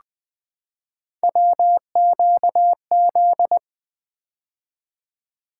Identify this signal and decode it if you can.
Morse code, heard as a keyed tone, about 20 wpm, 700 Hz. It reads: WQZ
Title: Three Letter-Number Combinations - 20wpm